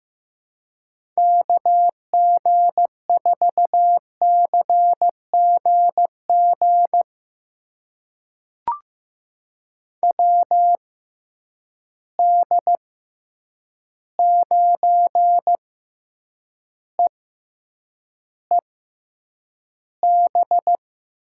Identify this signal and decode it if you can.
Morse code, heard as a keyed tone, about 15 wpm, spaced as for 8 wpm, 700 Hz. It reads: KG4CGG WD9EEB
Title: USA Call Signs - 6 Characters - 6x Character Spacing - 15wpm